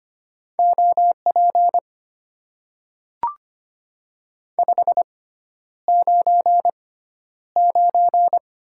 Morse code, heard as a keyed tone, about 25 wpm, spaced as for 11 wpm, 700 Hz. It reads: OP 599
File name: Top 90 QSO Elements - 6x Character Spacing - 25wpm